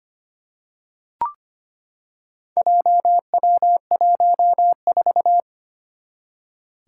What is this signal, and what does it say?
Morse code, heard as a keyed tone, about 25 wpm, 700 Hz. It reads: JW14
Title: Four Challenging Letter-Number Combinations 25wpm